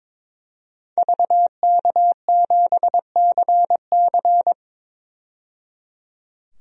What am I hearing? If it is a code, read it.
Morse code, heard as a keyed tone, about 22 wpm, 700 Hz. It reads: VK7CC
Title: DXCC-100 Call Signs - 5 Characters - 22wpm